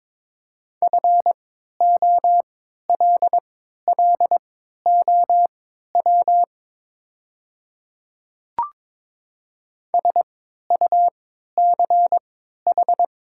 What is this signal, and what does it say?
Morse code, heard as a keyed tone, about 22 wpm, spaced as for 13 wpm, 700 Hz. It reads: FOLLOW SUCH
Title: 3x Character Spacing - Top 300 Words - 22wpm